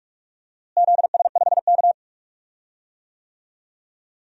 Morse code, heard as a keyed tone, about 46 wpm, 700 Hz. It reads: ZS5K
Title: DXCC-100 Call Signs - 4 Characters - Repeated 3x - 45wpm